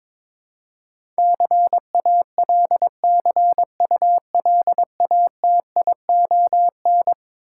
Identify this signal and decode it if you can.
Morse code, heard as a keyed tone, about 21 wpm, 700 Hz. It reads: CALCULATION
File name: practice / 22wpm / Eleven Letter Words - 22wpm